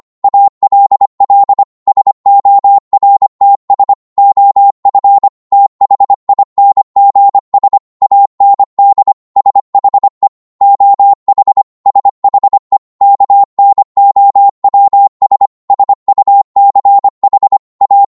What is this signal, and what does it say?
Morse code, heard as a keyed tone, about 25 wpm, 800 Hz. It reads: ALL SORTS OF THINGS AND SHE OH SHE KNOWS SUCH A